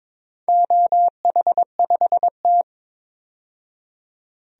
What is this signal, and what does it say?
Morse code, heard as a keyed tone, about 22 wpm, 700 Hz. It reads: OH5T
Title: DXCC-100 Call Signs - 4 Characters - 22wpm